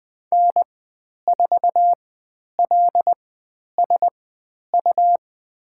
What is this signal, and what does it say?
Morse code, heard as a keyed tone, about 20 wpm, spaced as for 10 wpm, 700 Hz. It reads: N4LSU
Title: 109 Lesson 18g - Letter m - Call Signs 20wpm